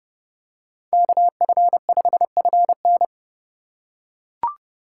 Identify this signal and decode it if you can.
Morse code, heard as a keyed tone, about 29 wpm, 700 Hz. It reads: KF5FN